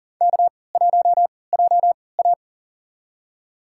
Morse code, heard as a keyed tone, about 40 wpm, spaced as for 23 wpm, 700 Hz. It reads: K1JA
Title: USA Call Signs - 4 Characters - 3x Character Spacing - 40wpm